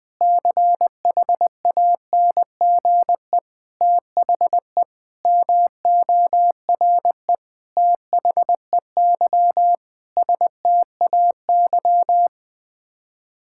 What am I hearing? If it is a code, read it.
Morse code, heard as a keyed tone, about 20 wpm, 700 Hz. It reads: CHANGE THE MORE THEY STAY
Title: Sets of 8 Words - 20wpm